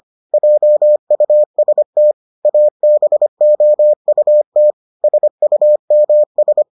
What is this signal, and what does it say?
Morse code, heard as a keyed tone, about 25 wpm, 600 Hz. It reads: JUST ABOUT SUMS